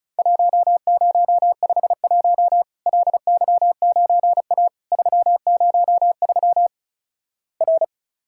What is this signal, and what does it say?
Morse code, heard as a keyed tone, about 34 wpm, 700 Hz. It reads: 1051 LY9A 303 R
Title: Worked All Europe DX Contest - QTC Exchange - 35wpm